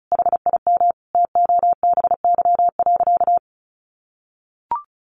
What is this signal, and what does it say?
Morse code, heard as a keyed tone, about 34 wpm, 700 Hz. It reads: HIM TOBY.
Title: Easy Phrases 35wpm